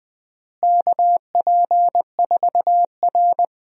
Morse code, heard as a keyed tone, about 20 wpm, 700 Hz. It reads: KP4R
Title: DXCC-100 Call Signs - 4 Characters - Repeated 3x - 20wpm